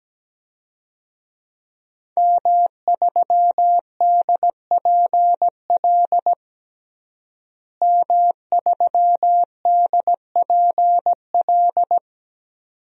Morse code, heard as a keyed tone, about 17 wpm, 700 Hz. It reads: M3DPL M3DPL
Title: DXCC-100 Call Signs - 5 Characters - Repeated 2x - 17wpm